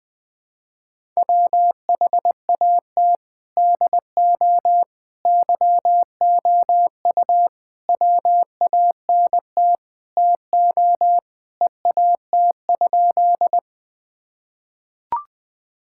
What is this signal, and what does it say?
Morse code, heard as a keyed tone, about 20 wpm, 700 Hz. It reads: WHAT DO YOU WANT TO EAT?
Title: Sentences with 5-Word Phrases - 20wpm